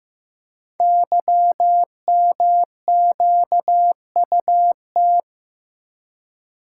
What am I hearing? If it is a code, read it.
Morse code, heard as a keyed tone, about 15 wpm, 700 Hz. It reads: YMQUT
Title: Code Groups - 5 Letters - 15wpm